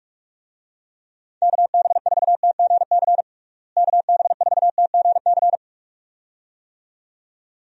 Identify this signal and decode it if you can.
Morse code, heard as a keyed tone, about 41 wpm, 700 Hz. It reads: KB4TGC KB4TGC
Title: USA Call Signs - 6 Characters - Repeated 2x - 45wpm